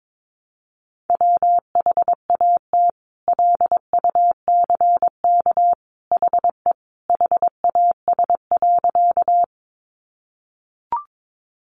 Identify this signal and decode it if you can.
Morse code, heard as a keyed tone, about 22 wpm, 700 Hz. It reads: WHAT LUCK HE HAS.
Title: Sentences from Top 1500 Words 22wpm